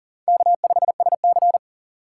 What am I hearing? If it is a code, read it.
Morse code, heard as a keyed tone, about 37 wpm, 700 Hz. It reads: K5SC
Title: USA Call Signs - 4 Characters - 6x Character Spacing - 40wpm